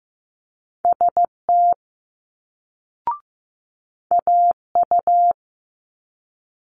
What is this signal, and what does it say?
Morse code, heard as a keyed tone, about 15 wpm, 700 Hz. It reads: ST AU
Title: Multi-letter Phonemes Quick Fire 15wpm